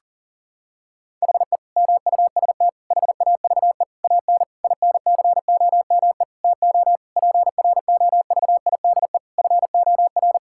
Morse code, heard as a keyed tone, about 40 wpm, 700 Hz. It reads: HE MUST HAVE AN INCOME TO PROVIDE FOR
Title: Sentences from Top 1000 Words - Review Part 5 - 40wpm